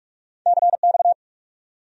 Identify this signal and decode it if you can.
Morse code, heard as a keyed tone, about 59 wpm, spaced as for 51 wpm, 700 Hz. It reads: CX
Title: Speed-Racing Abbreviations - 45wpm